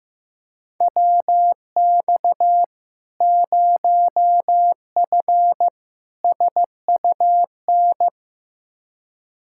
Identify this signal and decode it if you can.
Morse code, heard as a keyed tone, about 15 wpm, 700 Hz. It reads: WX 0F SUN
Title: Practice QSO - Exchange 2 - Info - 15wpm